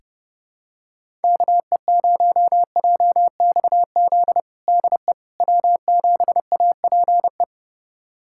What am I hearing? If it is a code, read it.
Morse code, heard as a keyed tone, about 30 wpm, 700 Hz. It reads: KE0JXZ DE W7APE
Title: USA Callsign DE Callsign 30wpm